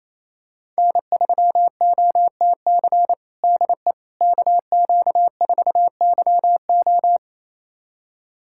Morse code, heard as a keyed tone, about 28 wpm, 700 Hz. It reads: N3OTC DE KQ4YO
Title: USA Callsign DE Callsign - 5 Characters - 28wpm